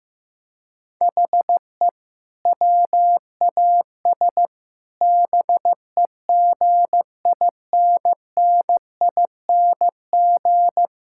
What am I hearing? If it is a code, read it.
Morse code, heard as a keyed tone, about 15 wpm, 700 Hz. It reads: HE WAS BEGINNING
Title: Sentences from Top 1000 Words - Review Part 5 - 15wpm